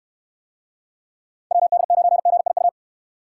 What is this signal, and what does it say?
Morse code, heard as a keyed tone, about 64 wpm, 700 Hz. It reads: WD9GIU